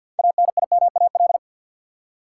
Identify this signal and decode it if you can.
Morse code, heard as a keyed tone, about 53 wpm, 700 Hz. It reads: ANIMAL
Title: Six Letter Words - 50wpm